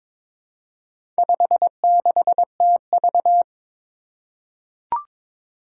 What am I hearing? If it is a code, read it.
Morse code, heard as a keyed tone, about 22 wpm, 700 Hz. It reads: 56TV
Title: Four Challenging Letter-Number Combinations 22wpm